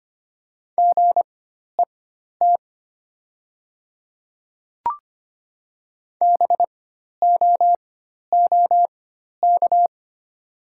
Morse code, heard as a keyed tone, about 25 wpm, spaced as for 12 wpm, 700 Hz. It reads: GET BOOK
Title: 4x Character Spacing - Top 300 Words - 25wpm